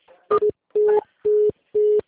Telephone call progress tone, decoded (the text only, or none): congestion tone